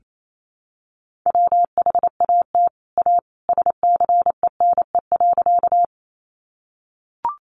Morse code, heard as a keyed tone, about 28 wpm, 700 Hz. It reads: WHAT A SCENE.
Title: Sentences from Top 1100 Words 28wpm